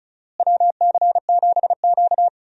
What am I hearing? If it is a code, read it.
Morse code, heard as a keyed tone, about 36 wpm, 700 Hz. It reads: WC7Q
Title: USA Call Signs - 4 Characters - Repeated 2x - 35wpm